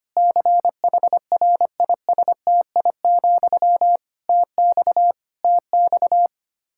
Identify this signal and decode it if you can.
Morse code, heard as a keyed tone, about 25 wpm, 700 Hz. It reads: CHRISTI, TX TX
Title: K1USN SST QSO Practice 25wpm